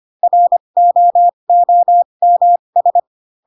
Morse code, heard as a keyed tone, about 25 wpm, spaced as for 22 wpm, 700 Hz. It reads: ROOMS